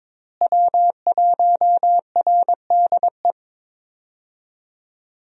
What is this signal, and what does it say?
Morse code, heard as a keyed tone, about 22 wpm, 700 Hz. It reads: W1RDE
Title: USA Call Signs - 5 Characters - Repeated 2x - 22wpm